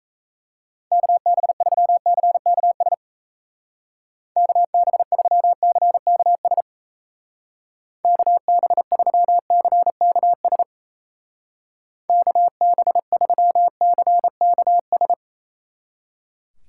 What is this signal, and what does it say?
Morse code, heard as a keyed tone, about 36 wpm, 700 Hz. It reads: KB3CKS KB3CKS KB3CKS KB3CKS